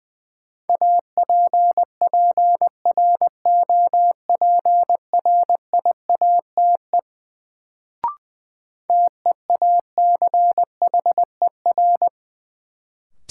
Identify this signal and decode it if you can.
Morse code, heard as a keyed tone, about 20 wpm, 700 Hz. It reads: APPROPRIATE TEACHER
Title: Sentences from Top 900 Words - New Word Drill - 20wpm